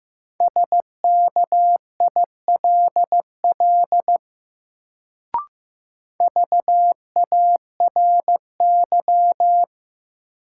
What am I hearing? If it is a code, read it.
Morse code, heard as a keyed tone, about 15 wpm, 700 Hz. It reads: SKILL VARY
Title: Sentences from Top 1200 Words - New Word Drill Alt - 15wpm